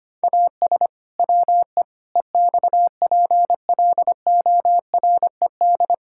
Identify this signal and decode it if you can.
Morse code, heard as a keyed tone, about 25 wpm, 700 Hz. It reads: AS WE EXPLORED
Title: Pangram Challenge v2 - 25wpm